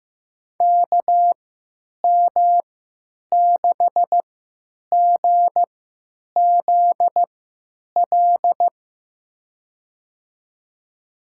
Morse code, heard as a keyed tone, about 15 wpm, spaced as for 9 wpm, 700 Hz. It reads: KM6GZL
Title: USA Call Signs - 6 Characters - 3x Character Spacing - 15wpm